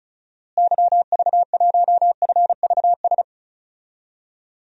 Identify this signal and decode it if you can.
Morse code, heard as a keyed tone, about 36 wpm, 700 Hz. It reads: YV1FVS
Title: DXCC-100 Call Signs - 6 Characters - Repeated 2x - 35wpm